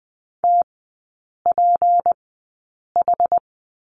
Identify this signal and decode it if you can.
Morse code, heard as a keyed tone, about 20 wpm, spaced as for 8 wpm, 700 Hz. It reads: TPH
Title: 152 Lesson 25e - Letter p - Three Characters 20wpm